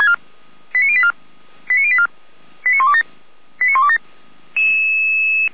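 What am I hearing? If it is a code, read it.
Five-tone selective call (ZVEI), heard as quick runs of five tones, tones of about 70 ms. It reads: 89084 89084 89127 89127
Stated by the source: ZVEI five-tone selective call received by SDR, fire brigade Orlamünde (feuerwehr-orlamuende), address 89235.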